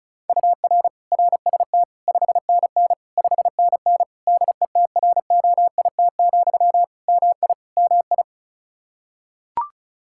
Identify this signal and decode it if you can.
Morse code, heard as a keyed tone, about 34 wpm, 700 Hz. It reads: UR RST 5NN 5NN DETROIT, MI MI
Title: Practice QSO - Exchange 1 - Beginning - 35wpm